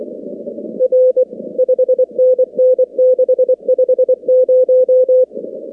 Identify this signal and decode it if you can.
Morse code, heard as a keyed tone, about 24 wpm, 510 Hz. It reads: R 5NN 650
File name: contests / 2019-cw-wpx-kt5j-k5pi- -2330 0-19